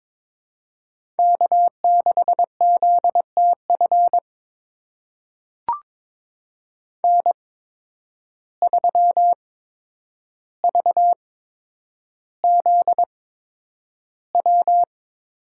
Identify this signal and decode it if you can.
Morse code, heard as a keyed tone, about 22 wpm, spaced as for 9 wpm, 700 Hz. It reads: K6ZTF N3VZW
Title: USA Call Signs - 5 Characters - 8x Character Spacing - 22wpm